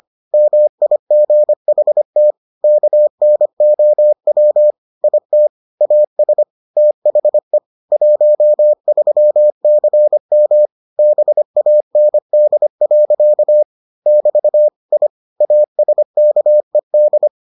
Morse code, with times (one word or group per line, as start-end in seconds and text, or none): 0.3-2.3 MIGHT
2.6-4.7 KNOW
5.0-5.5 IT
5.8-6.4 AS
6.8-7.6 THE
7.9-10.7 13CM
11.0-13.6 BAND.
14.1-14.7 <BT>
14.9-15.1 I
15.4-17.3 ASKED